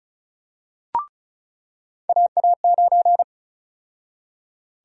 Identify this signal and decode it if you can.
Morse code, heard as a keyed tone, about 36 wpm, 700 Hz. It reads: AA9
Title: USA Call Sign Prefixes - 35wpm